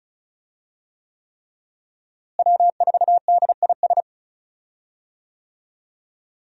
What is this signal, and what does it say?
Morse code, heard as a keyed tone, about 34 wpm, 700 Hz. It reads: W4DIS
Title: USA Call Signs - 5 Characters - Repeated 3x - 35wpm